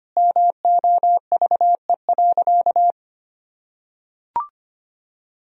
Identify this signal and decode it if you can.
Morse code, heard as a keyed tone, about 25 wpm, 700 Hz. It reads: MOVE.
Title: Sentences from Top 1000 Words - Review Part 1 - 25wpm